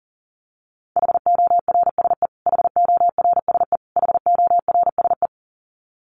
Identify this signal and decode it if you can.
Morse code, heard as a keyed tone, about 40 wpm, 700 Hz. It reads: HORSE HORSE HORSE
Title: Top 300 Words Repeated 40wpm